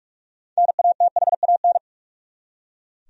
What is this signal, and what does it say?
Morse code, heard as a keyed tone, about 41 wpm, 700 Hz. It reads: NATHAN
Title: USA Common Names - 45wpm